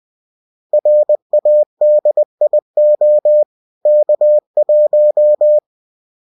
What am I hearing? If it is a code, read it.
Morse code, heard as a keyed tone, about 20 wpm, 600 Hz. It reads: RADIO K1